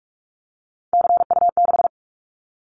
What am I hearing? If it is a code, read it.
Morse code, heard as a keyed tone, about 46 wpm, 700 Hz. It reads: CU6
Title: Callsign Prefixes for Top 50 DXCC - 45wpm